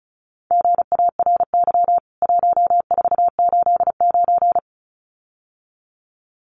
Morse code, heard as a keyed tone, about 34 wpm, 700 Hz. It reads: GARY 1489
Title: CWT Exchange 35wpm